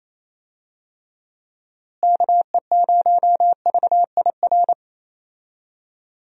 Morse code, heard as a keyed tone, about 28 wpm, 700 Hz. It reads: KE0VIR
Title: USA Call Signs - 6 Characters - Repeated 3x - 28wpm